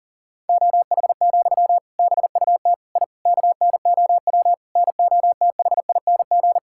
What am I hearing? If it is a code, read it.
Morse code, heard as a keyed tone, about 40 wpm, 700 Hz. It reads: OH, BUT I KNOW NOTHING
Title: Sentences from Top 1000 Words - Review Part 6 - 40wpm